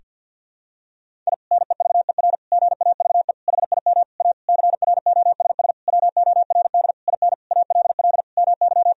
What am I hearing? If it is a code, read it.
Morse code, heard as a keyed tone, about 50 wpm, 700 Hz. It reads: I NEVER GAVE HIM A CROSS WORD IN ALL MY
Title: Sentences from Top 1000 Words - Review Part 1 - 50wpm